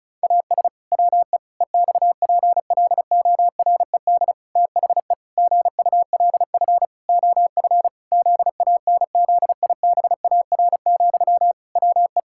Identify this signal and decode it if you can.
Morse code, heard as a keyed tone, about 34 wpm, 700 Hz. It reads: AS WE EXPLORED THE GULF OF ZANZIBAR, WE
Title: Pangram Challenge v2 - 35wpm